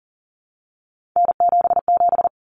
Morse code, heard as a keyed tone, about 40 wpm, 700 Hz. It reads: N77
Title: Common Cut Numbers - 3-Digit 40wpm